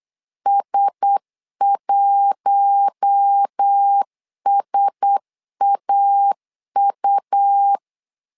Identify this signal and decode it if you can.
Morse code, heard as a keyed tone, about 8 wpm, 800 Hz. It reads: S1SAU
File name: LEZIONI / 4 Q1S 8wpm 2710